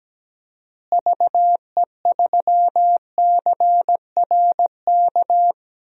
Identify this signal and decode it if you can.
Morse code, heard as a keyed tone, about 17 wpm, 700 Hz. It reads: VE3CRK
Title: DXCC-100 Call Signs - 6 Characters - Repeated 3x - 17wpm